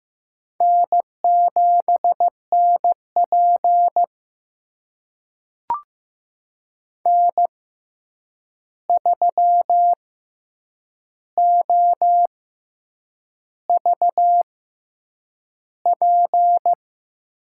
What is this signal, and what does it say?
Morse code, heard as a keyed tone, about 15 wpm, spaced as for 7 wpm, 700 Hz. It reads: N7NP N3OVP
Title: USA Call Signs - 6x Character Spacing - 15wpm